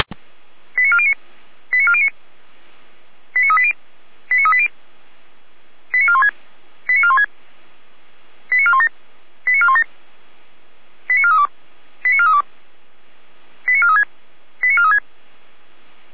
Five-tone selective call (ZVEI), heard as quick runs of five tones, tones of about 70 ms. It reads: 89409 89409 89380 89380 89416 89416 89417 89417 89432 89432 89436 89436